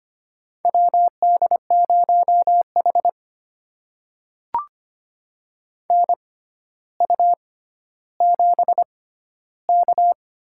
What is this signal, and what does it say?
Morse code, heard as a keyed tone, about 25 wpm, spaced as for 13 wpm, 700 Hz. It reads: WD0H NU7K
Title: USA Call Signs - 4 Characters - 6x Character Spacing - 25wpm